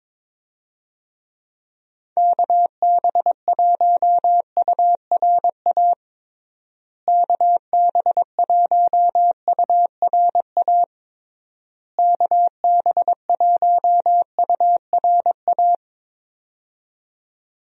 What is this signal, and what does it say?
Morse code, heard as a keyed tone, about 22 wpm, 700 Hz. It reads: KB1URA KB1URA KB1URA
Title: USA Call Signs - 6 Characters - Repeated 3x - 22wpm